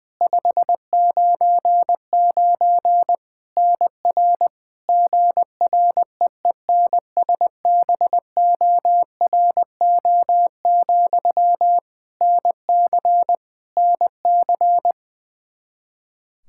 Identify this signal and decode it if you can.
Morse code, heard as a keyed tone, about 20 wpm, 700 Hz. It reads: 599 NR GREENSBORO, NC NC
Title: Practice QSO - Exchange 1 - Beginning - 20wpm